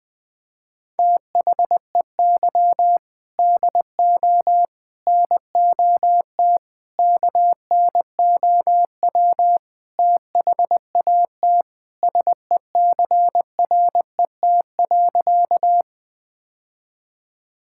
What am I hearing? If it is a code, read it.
Morse code, heard as a keyed tone, about 21 wpm, 700 Hz. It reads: THEY DO NOT KNOW THAT SECRET.
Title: Sentences with 4-Word Phrases - 20wpm